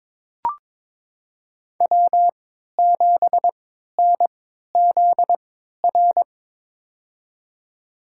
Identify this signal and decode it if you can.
Morse code, heard as a keyed tone, about 22 wpm, spaced as for 13 wpm, 700 Hz. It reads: W7NZR
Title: USA Call Signs - 5 Characters - 3x Character Spacing - 22wpm